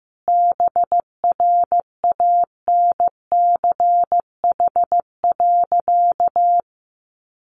Morse code, heard as a keyed tone, about 15 wpm, 700 Hz. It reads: BRANCH.
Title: A Sound Of Thunder 15wpm